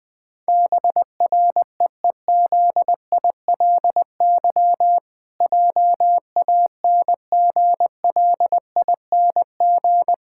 Morse code, heard as a keyed tone, about 20 wpm, 700 Hz. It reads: BREEZILY JANGLING
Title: Pangram Challenge v2 - 20wpm